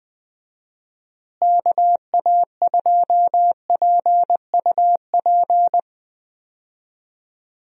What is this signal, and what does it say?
Morse code, heard as a keyed tone, about 20 wpm, 700 Hz. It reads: KA2PUP
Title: USA Call Signs - 6 Characters - Repeated 3x - 20wpm